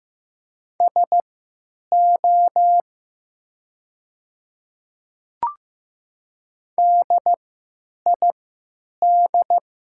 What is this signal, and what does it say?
Morse code, heard as a keyed tone, about 15 wpm, spaced as for 9 wpm, 700 Hz. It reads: SO DID
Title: 3x Character Spacing - Top 300 Words - 15wpm